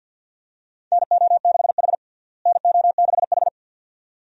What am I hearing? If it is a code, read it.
Morse code, heard as a keyed tone, about 49 wpm, 700 Hz. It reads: NO6H NO6H
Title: USA Call Signs - 4 Characters - Repeated 2x - 50wpm